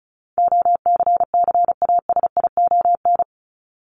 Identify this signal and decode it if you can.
Morse code, heard as a keyed tone, about 34 wpm, 700 Hz. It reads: OCCASION
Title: Top 1501-1750 Words 35wpm